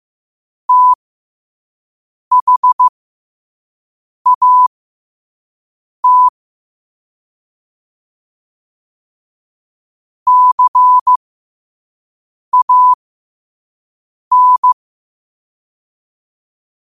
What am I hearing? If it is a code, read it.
Morse code, heard as a keyed tone, about 15 wpm, spaced as for 5 wpm, 1000 Hz. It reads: THAT CAN
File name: qotdfiles / 5wpm